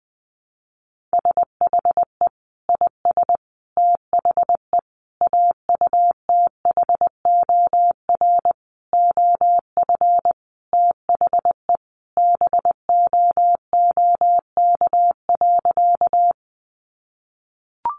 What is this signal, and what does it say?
Morse code, heard as a keyed tone, about 20 wpm, 700 Hz. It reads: SHE IS THE AUTHOR OF THE BOOK.
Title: Sentences from Top 1200 Words 20wpm